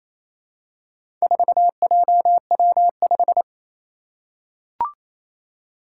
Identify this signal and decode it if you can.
Morse code, heard as a keyed tone, about 28 wpm, 700 Hz. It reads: 4JW5
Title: Four Challenging Letter-Number Combinations 28wpm